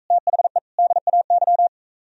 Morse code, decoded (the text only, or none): THE DAY